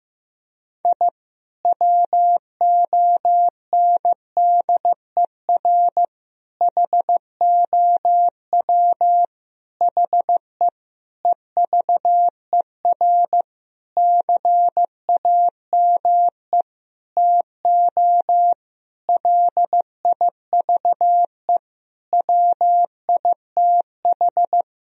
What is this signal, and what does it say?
Morse code, heard as a keyed tone, about 15 wpm, 700 Hz. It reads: I WONDER HOW HE EVER CAME TO LIVE WITH
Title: Sentences from Top 1000 Words - Review Part 3 - 15wpm